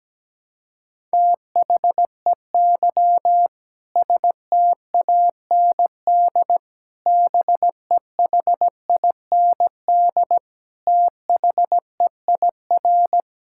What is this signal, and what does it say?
Morse code, heard as a keyed tone, about 17 wpm, 700 Hz. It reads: THEY STAND BEHIND THEIR
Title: Sentences from Top 1000 Words - Review Part 5 - 17wpm